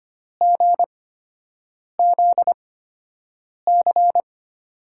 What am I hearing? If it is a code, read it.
Morse code, heard as a keyed tone, about 25 wpm, spaced as for 7 wpm, 700 Hz. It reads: GZC
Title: USA Call Signs - 8x Character Spacing - 25wpm